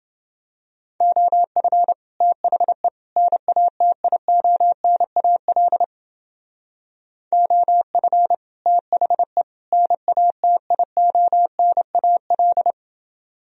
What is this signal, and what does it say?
Morse code, heard as a keyed tone, about 30 wpm, 700 Hz. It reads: OF THE NATIONAL OF THE NATIONAL